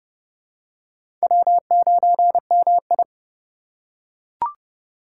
Morse code, heard as a keyed tone, about 30 wpm, 700 Hz. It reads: W9MI
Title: USA Call Signs - 4 Characters - 2x Character Spacing - 30wpm